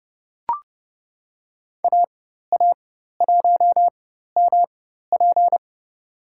Morse code, heard as a keyed tone, about 30 wpm, spaced as for 14 wpm, 700 Hz. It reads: AA1MP
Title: USA Call Signs - 5 Characters - 4x Character Spacing - 30wpm